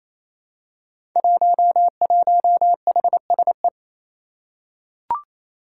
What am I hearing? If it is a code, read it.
Morse code, heard as a keyed tone, about 28 wpm, 700 Hz. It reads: 11HSE
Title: Five Challenging Letter-Number Combinations 28wpm